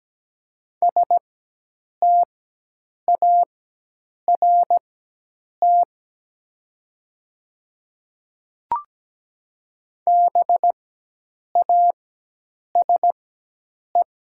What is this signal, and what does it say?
Morse code, heard as a keyed tone, about 17 wpm, spaced as for 8 wpm, 700 Hz. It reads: START BASE
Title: 4x Character Spacing - Top 300 Words - 17wpm